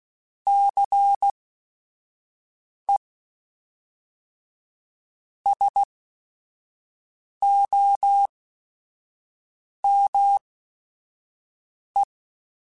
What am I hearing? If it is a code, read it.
Morse code, heard as a keyed tone, about 16 wpm, spaced as for 5 wpm, 800 Hz. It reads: CE SOME
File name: QSOs / QSOs-mp3s / WD0BGP 1@5